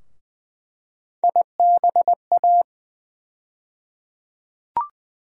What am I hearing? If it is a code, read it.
Morse code, heard as a keyed tone, about 20 wpm, 700 Hz. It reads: IBA
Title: USA Call Sign Postfixes - 3 Letters 20wpm